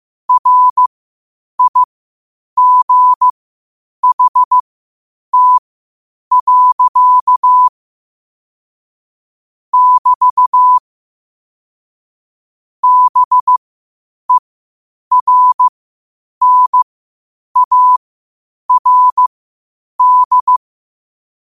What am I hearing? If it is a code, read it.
Morse code, heard as a keyed tone, about 15 wpm, spaced as for 8 wpm, 1000 Hz. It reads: RIGHT. <BT> BERNARD
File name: qotdfiles / 8wpm